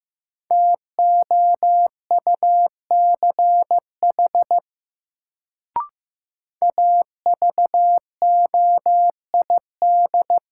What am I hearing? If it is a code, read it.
Morse code, heard as a keyed tone, about 15 wpm, 700 Hz. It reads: TOUCH AVOID
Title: Sentences from Top 1200 Words - New Word Drill Alt - 15wpm